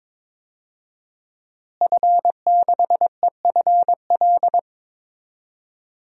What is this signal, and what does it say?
Morse code, heard as a keyed tone, about 21 wpm, 700 Hz. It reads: F6EFL
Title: DXCC-100 Call Signs - 5 Characters - Repeated 2x - 22wpm